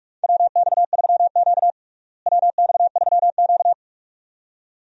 Morse code, heard as a keyed tone, about 46 wpm, 700 Hz. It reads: WX3Q WX3Q
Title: USA Call Signs - 4 Characters - Repeated 2x - 45wpm